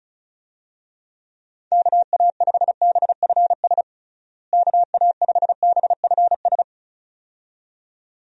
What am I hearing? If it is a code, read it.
Morse code, heard as a keyed tone, about 36 wpm, 700 Hz. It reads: KA5BFS KA5BFS
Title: USA Call Signs - 6 Characters - Repeated 2x - 35wpm